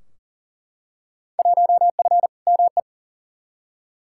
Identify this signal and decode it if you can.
Morse code, heard as a keyed tone, about 40 wpm, 700 Hz. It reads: 1F ME
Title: Field Day Exchanges Encore - 40wpm